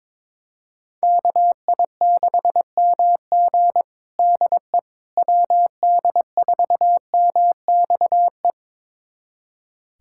Morse code, heard as a keyed tone, about 22 wpm, 700 Hz. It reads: KI6MG DE WD4MXE
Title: USA Callsign DE Callsign 22wpm